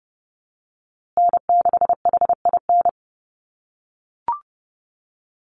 Morse code, heard as a keyed tone, about 30 wpm, 700 Hz. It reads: N6HIN
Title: USA Call Signs Repeated 2x 30wpm